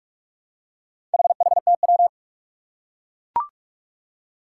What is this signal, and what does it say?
Morse code, heard as a keyed tone, about 42 wpm, 700 Hz. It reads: HHTW
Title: Four Challenging Letter-Number Combinations 45wpm